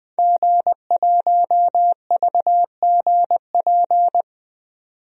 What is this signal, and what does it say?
Morse code, heard as a keyed tone, about 21 wpm, 700 Hz. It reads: G1VGP
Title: DXCC-100 Call Signs - 5 Characters - 20wpm